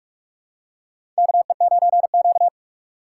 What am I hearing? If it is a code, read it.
Morse code, heard as a keyed tone, about 43 wpm, 700 Hz. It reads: KE9Q
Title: USA Call Signs - 4 Characters - 8x Character Spacing - 45wpm